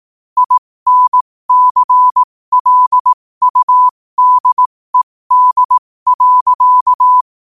Morse code, heard as a keyed tone, about 18 wpm, spaced as for 15 wpm, 1000 Hz. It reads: INCLUDED.
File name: qotdfiles / 15wpm